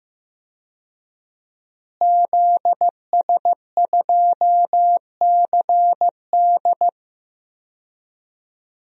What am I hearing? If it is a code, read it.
Morse code, heard as a keyed tone, about 15 wpm, 700 Hz. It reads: ZS2CD
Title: DXCC-100 Call Signs - 5 Characters - Repeated 2x - 15wpm